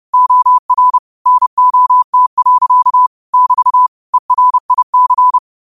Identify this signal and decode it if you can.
Morse code, heard as a keyed tone, about 30 wpm, 1000 Hz. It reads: OR NOT. <BT> ERIC
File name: qotdfiles / 30wpm